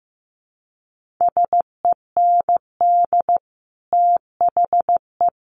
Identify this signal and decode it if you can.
Morse code, heard as a keyed tone, about 15 wpm, 700 Hz. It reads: SEND THE
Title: Sentences from Top 1500 Words 15wpm